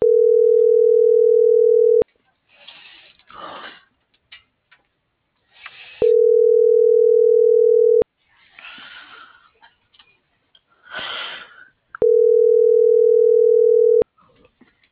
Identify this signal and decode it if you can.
Ringback tone (North America), 440 + 480 Hz.